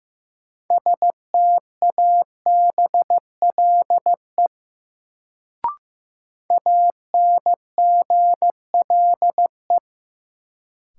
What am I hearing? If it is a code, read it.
Morse code, heard as a keyed tone, about 15 wpm, 700 Hz. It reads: STABLE ANGLE
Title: Sentences from Top 1500 Words - New Word Drill Alt - 15wpm